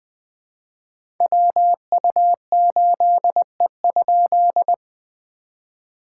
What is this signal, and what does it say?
Morse code, heard as a keyed tone, about 20 wpm, 700 Hz. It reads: WU8E?
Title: POTA QSO Practice - 20wpm